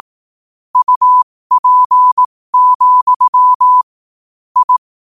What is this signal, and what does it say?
Morse code, heard as a keyed tone, about 18 wpm, spaced as for 15 wpm, 1000 Hz. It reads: UP, I
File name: qotdfiles / 15wpm